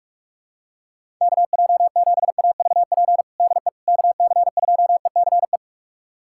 Morse code, heard as a keyed tone, about 46 wpm, 700 Hz. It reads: KJ7AVP DE KC2ECE
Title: USA Callsign DE Callsign 45wpm